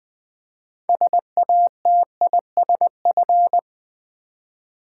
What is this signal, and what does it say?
Morse code, heard as a keyed tone, about 20 wpm, 700 Hz. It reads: SATISF
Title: Build A Word - 6 Letters At A Time - 12-Letter Words - 20wpm